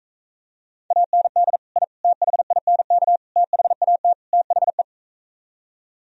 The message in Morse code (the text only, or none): AND I THINK THAT THE